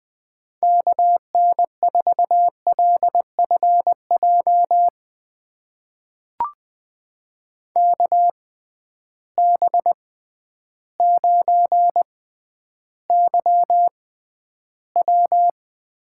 Morse code, heard as a keyed tone, about 20 wpm, spaced as for 11 wpm, 700 Hz. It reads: KN4LFJ KB9YW
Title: USA Call Signs - 6 Characters - 6x Character Spacing - 20wpm